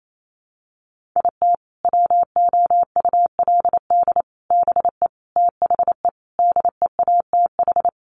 Morse code, heard as a keyed tone, about 28 wpm, 700 Hz. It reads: IT WOULD BE THE DEATH
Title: Sentences from Top 1400 Words 28wpm